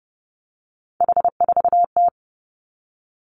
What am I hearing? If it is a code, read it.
Morse code, heard as a keyed tone, about 30 wpm, 700 Hz. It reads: H4T
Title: Three Challenging Letter-Number Combinations 30wpm